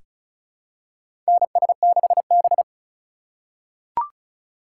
Morse code, heard as a keyed tone, about 36 wpm, 700 Hz. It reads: NS6B
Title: USA Call Signs - 4 Characters - 6x Character Spacing - 35wpm